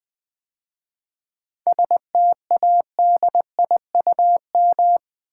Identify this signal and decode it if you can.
Morse code, heard as a keyed tone, about 20 wpm, 700 Hz. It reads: STADIUM
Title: 7 Letter Words - First 3 Letters Given - 20wpm